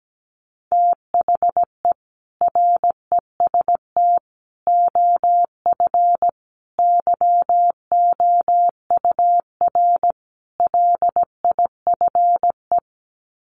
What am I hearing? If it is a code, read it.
Morse code, heard as a keyed tone, about 17 wpm, 700 Hz. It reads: THE REST OF YOUR LIFE
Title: Sets of 5 Words 17wpm